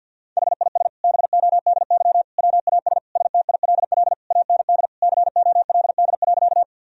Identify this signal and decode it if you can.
Morse code, heard as a keyed tone, about 52 wpm, 700 Hz. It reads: HIS BODY WAS STILL AND COLD.
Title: Sentences from Top 1000 Words - Review Part 2 - 50wpm